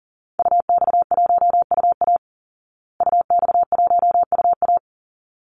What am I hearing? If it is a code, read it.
Morse code, heard as a keyed tone, about 37 wpm, 700 Hz. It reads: UX1UA UX1UA
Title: Call Signs Repeated 3x - 40wpm